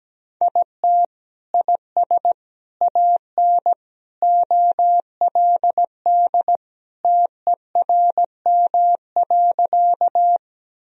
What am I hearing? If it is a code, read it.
Morse code, heard as a keyed tone, about 17 wpm, 700 Hz. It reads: IT IS AN OLD TERM.
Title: Sentences from Top 1000 Words - Review Part 4 - 17wpm